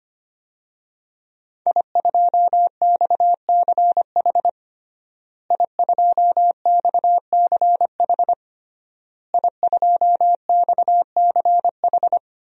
Morse code, heard as a keyed tone, about 25 wpm, 700 Hz. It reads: I2XCH I2XCH I2XCH
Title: DXCC-100 Call Signs - 5 Characters - Repeated 3x - 25wpm